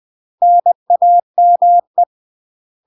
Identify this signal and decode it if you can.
Morse code, heard as a keyed tone, about 20 wpm, 700 Hz. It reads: NAME
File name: UK MASTER QSOs TRAINING / 20WPM / UK MASTER QSOs 0074